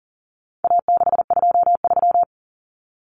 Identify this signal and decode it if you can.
Morse code, heard as a keyed tone, about 40 wpm, 700 Hz. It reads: A623
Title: Common Cut Numbers - 4-Digit 40wpm